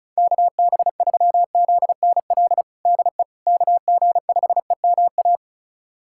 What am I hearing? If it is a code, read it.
Morse code, heard as a keyed tone, about 36 wpm, 700 Hz. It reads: KB3ZNL DE KG5EMA
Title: USA Callsign DE Callsign - 6 Characters - 35wpm